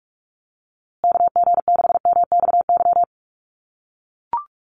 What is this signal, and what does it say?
Morse code, heard as a keyed tone, about 44 wpm, 700 Hz. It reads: KG6MXY
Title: USA Call Signs Repeated 3x 45wpm